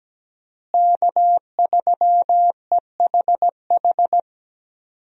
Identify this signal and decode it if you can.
Morse code, heard as a keyed tone, about 17 wpm, 700 Hz. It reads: K3EHH
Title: USA Call Signs - 5 Characters - Repeated 3x - 17wpm